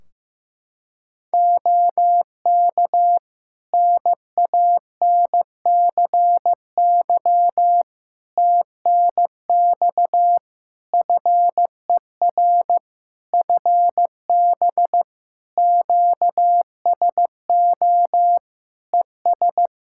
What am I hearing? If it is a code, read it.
Morse code, heard as a keyed tone, about 15 wpm, 700 Hz. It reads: OK NANCY TNX FER FB QSO ES
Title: Practice QSO - Exchange 4 - Ending - 15wpm